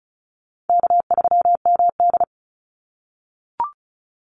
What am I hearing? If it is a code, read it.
Morse code, heard as a keyed tone, about 34 wpm, 700 Hz. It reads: K3MD
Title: Call Signs - 35wpm